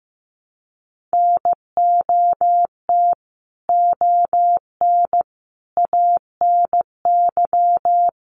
Morse code, heard as a keyed tone, about 15 wpm, 700 Hz. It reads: NOT ON ANY
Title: Sentences from Top 900 Words 15wpm